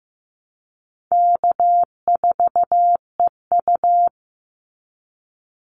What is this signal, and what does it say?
Morse code, heard as a keyed tone, about 15 wpm, 700 Hz. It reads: K4EU
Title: Easy Call Signs - 15wpm